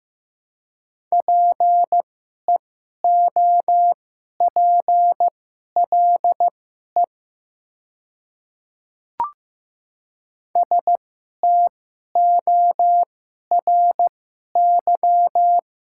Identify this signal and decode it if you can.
Morse code, heard as a keyed tone, about 15 wpm, spaced as for 11 wpm, 700 Hz. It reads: PEOPLE STORY
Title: 2x Character Spacing - Top 300 Words - 15wpm